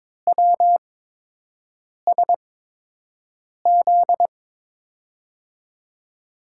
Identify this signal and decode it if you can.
Morse code, heard as a keyed tone, about 21 wpm, spaced as for 6 wpm, 700 Hz. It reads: WSZ
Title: USA Call Signs - 8x Character Spacing - 22wpm